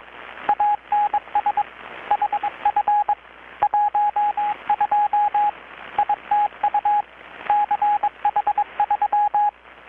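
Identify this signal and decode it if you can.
Morse code, heard as a keyed tone, about 22 wpm, 830 Hz, with radio noise. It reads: ANS HF 12 ITU CH3
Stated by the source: Morse from KPH's Night of Nights XX maritime radiotelegraphy event, received in Chiapas, Mexico about 3,800 km away on a Kenwood R-600 with a grounded 10 m long-wire antenna.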